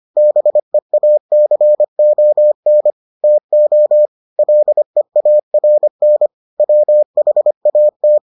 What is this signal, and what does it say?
Morse code, heard as a keyed tone, about 25 wpm, 600 Hz. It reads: BEACON TO LEARN WHAT